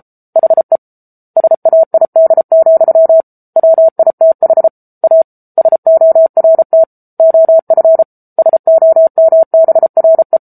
Morse code, heard as a keyed tone, about 34 wpm, 650 Hz. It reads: HE SAID, WITH A SORT OF SOMBRE
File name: Book Two / 07-Chapter 7 - 33 WPM